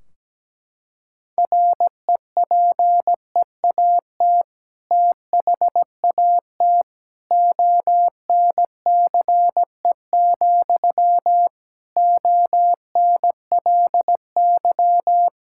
Morse code, heard as a keyed tone, about 17 wpm, 700 Hz. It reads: REPEAT THAT ONCE, ONLY
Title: Sentences from Top 1600 Words 17wpm